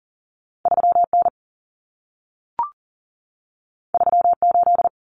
Morse code, heard as a keyed tone, about 40 wpm, 700 Hz. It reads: 3N 38
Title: Common Cut Numbers - 2-Digit 40wpm